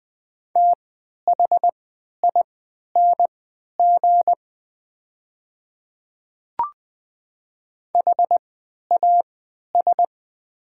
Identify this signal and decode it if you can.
Morse code, heard as a keyed tone, about 20 wpm, spaced as for 11 wpm, 700 Hz. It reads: THING HAS
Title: 3x Character Spacing - Top 100 Words - 20wpm